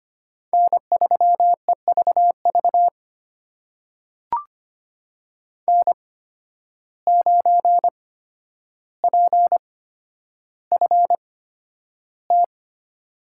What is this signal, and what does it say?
Morse code, heard as a keyed tone, about 25 wpm, spaced as for 11 wpm, 700 Hz. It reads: N3EVV N9PFT
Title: USA Call Signs - 5 Characters - 8x Character Spacing - 25wpm